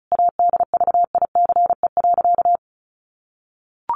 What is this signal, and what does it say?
Morse code, heard as a keyed tone, about 34 wpm, 700 Hz. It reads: ADVICE.
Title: Sentences from Top 1100 Words 35wpm